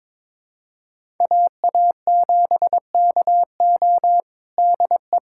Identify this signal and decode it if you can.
Morse code, heard as a keyed tone, about 21 wpm, 700 Hz. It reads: AA7KO DE
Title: USA Callsign DE Callsign - 5 Characters - 22wpm